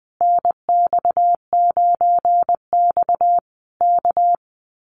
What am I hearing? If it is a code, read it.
Morse code, heard as a keyed tone, about 20 wpm, 700 Hz. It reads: NX9X K
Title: Common QSO Phrases Encore 20wpm